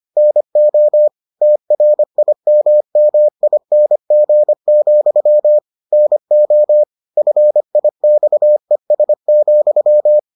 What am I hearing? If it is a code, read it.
Morse code, heard as a keyed tone, about 25 wpm, 600 Hz. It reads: NO TRIMMING, NO FIXES,